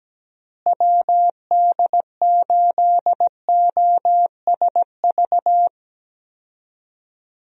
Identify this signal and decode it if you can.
Morse code, heard as a keyed tone, about 17 wpm, 700 Hz. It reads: WD8OSV
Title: USA Call Signs - 6 Characters - Repeated 3x - 17wpm